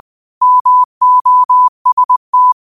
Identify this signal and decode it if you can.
Morse code, heard as a keyed tone, about 20 wpm, 1000 Hz. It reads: MOST
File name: qotdfiles / 20wpm